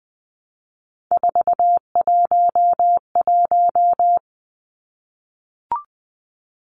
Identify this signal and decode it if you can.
Morse code, heard as a keyed tone, about 20 wpm, 700 Hz. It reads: 411
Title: Numbers - 3-Digit 20wpm